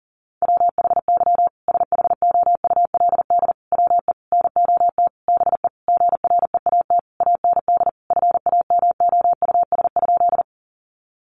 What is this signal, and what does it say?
Morse code, heard as a keyed tone, about 37 wpm, 700 Hz. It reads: WHY SHOULD WE NOT BE GREAT AND FAMOUS?
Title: Sentences from Top 1500 Words 40wpm